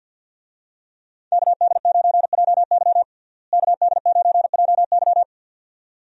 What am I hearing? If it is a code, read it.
Morse code, heard as a keyed tone, about 51 wpm, 700 Hz. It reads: KD9JY KD9JY
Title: USA Call Signs - 5 Characters - Repeated 2x - 50wpm